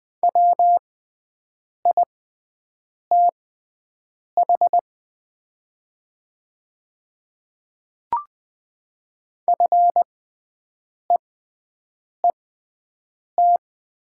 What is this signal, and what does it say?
Morse code, heard as a keyed tone, about 20 wpm, spaced as for 7 wpm, 700 Hz. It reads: WITH FEET
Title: 6x Character Spacing - Top 300 Words - 20wpm